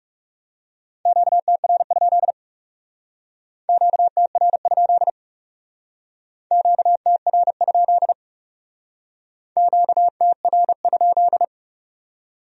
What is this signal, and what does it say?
Morse code, heard as a keyed tone, about 35 wpm, 700 Hz. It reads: QTR? QTR? QTR? QTR?
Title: Speed-Racing Q-Codes - 30wpm